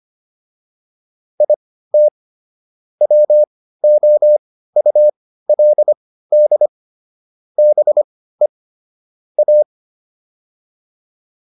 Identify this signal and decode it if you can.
Morse code, heard as a keyed tone, about 25 wpm, spaced as for 15 wpm, 600 Hz. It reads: IT WOULD BE A